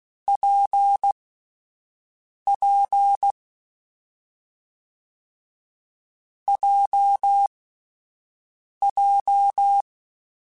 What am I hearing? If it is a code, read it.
Morse code, heard as a keyed tone, about 16 wpm, spaced as for 5 wpm, 800 Hz. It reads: PP JJ
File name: Difficult Combos / Difficult Combos-mp3s / WPJ WPJ1 @5